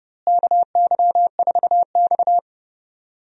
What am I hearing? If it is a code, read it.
Morse code, heard as a keyed tone, about 30 wpm, 700 Hz. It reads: KY4X
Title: USA Call Signs - 4 Characters - Repeated 3x - 30wpm